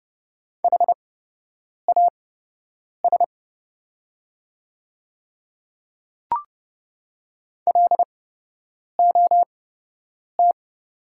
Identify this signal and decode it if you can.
Morse code, heard as a keyed tone, about 30 wpm, spaced as for 8 wpm, 690 Hz. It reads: HAS LOT
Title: 8x Character Spacing - Top 500 Words - 30wpm